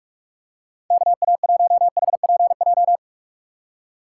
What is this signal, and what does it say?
Morse code, heard as a keyed tone, about 43 wpm, 700 Hz. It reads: KA1HPJ
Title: USA Call Signs - 6 Characters - 6x Character Spacing - 45wpm